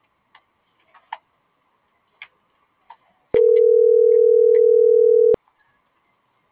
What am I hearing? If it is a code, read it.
Ringback tone (North America), 440 + 480 Hz.